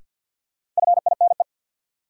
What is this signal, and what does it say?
Morse code, heard as a keyed tone, about 45 wpm, 700 Hz. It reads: FINE
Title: Sentences from Top 500 Words - New Word Drill - 50wpm